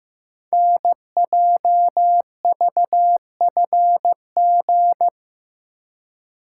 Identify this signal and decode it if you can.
Morse code, heard as a keyed tone, about 15 wpm, 690 Hz. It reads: NJVFG
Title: Code Groups - 5 Letters - 15wpm